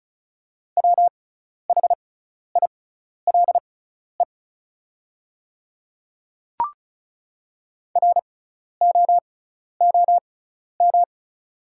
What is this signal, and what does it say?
Morse code, heard as a keyed tone, about 36 wpm, spaced as for 12 wpm, 700 Hz. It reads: WHILE ROOM
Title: 6x Character Spacing - Top 500 Words - 35wpm